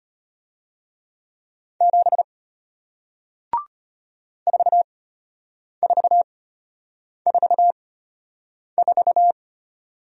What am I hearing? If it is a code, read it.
Morse code, heard as a keyed tone, about 33 wpm, spaced as for 8 wpm, 700 Hz. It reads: 7 4444